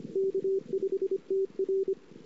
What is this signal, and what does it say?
Morse code, heard as a keyed tone, about 28 wpm, 380 Hz. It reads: K5TR
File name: CW / 2008-ss-cw-k5tr-op-n5rz -22:00